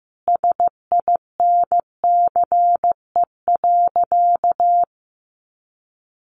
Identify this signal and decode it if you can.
Morse code, heard as a keyed tone, about 15 wpm, 700 Hz. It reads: SINCE.
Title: Sentences from Top 300 Words 15wpm